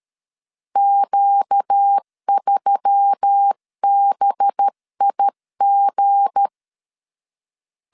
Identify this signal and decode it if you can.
Morse code, heard as a keyed tone, about 13 wpm, 800 Hz. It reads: Q3BIG
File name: LEZIONI / 8 YJ45 / 8 YJ45 12wpm 2151